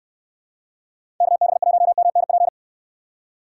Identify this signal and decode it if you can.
Morse code, heard as a keyed tone, about 69 wpm, 700 Hz. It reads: KB1MNP